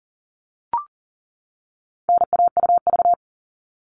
Morse code, heard as a keyed tone, about 40 wpm, 700 Hz. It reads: NAUV
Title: All Cut Numbers - 4-Digit 40wpm